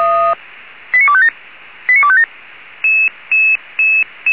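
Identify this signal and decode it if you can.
Five-tone selective call (ZVEI), heard as quick runs of five tones, tones of about 70 ms. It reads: 89257 89257
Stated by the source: ZVEI five-tone selective call received by SDR, from fire brigade Feuerwehr Orlamünde.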